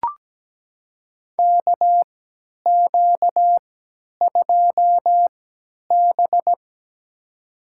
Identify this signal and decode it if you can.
Morse code, heard as a keyed tone, about 17 wpm, spaced as for 10 wpm, 700 Hz. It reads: KQ2B
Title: USA Call Signs - 4 Characters - 3x Character Spacing - 17wpm